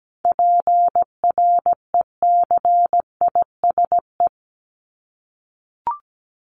Morse code, heard as a keyed tone, about 17 wpm, 700 Hz. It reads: PRECISE
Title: LY Suffix - Top 100 Words with Base Words 17wpm